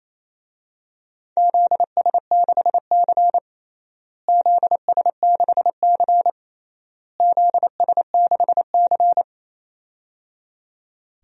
Morse code, heard as a keyed tone, about 28 wpm, 700 Hz. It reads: ZS6C ZS6C ZS6C
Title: DXCC-100 Call Signs - 4 Characters - Repeated 3x - 28wpm